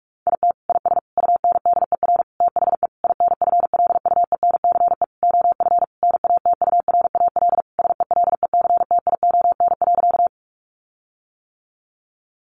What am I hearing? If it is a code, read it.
Morse code, heard as a keyed tone, about 44 wpm, 700 Hz. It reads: IT IS UNDER THE INFLUENCE OF NATURAL SELECTION.
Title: Sentences from Top 1400 Words 45wpm